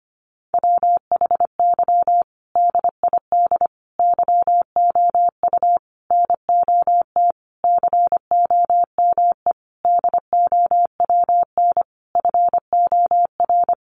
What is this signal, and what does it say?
Morse code, heard as a keyed tone, about 25 wpm, 700 Hz. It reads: WHY DID YOU NOT COME DOWN FOR
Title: Sentences from Top 1500 Words 25wpm